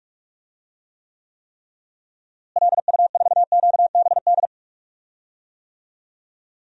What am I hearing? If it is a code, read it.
Morse code, heard as a keyed tone, about 41 wpm, 700 Hz. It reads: LU4QBD
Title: DXCC-100 Call Signs - 6 Characters - Repeated 3x - 45wpm